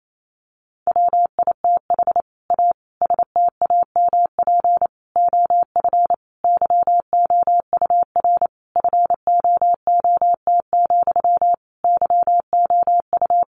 Morse code, heard as a keyed tone, about 28 wpm, 700 Hz. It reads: WITH A STAMP OF YOUR FOOT, YOU
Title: A Sound Of Thunder 28wpm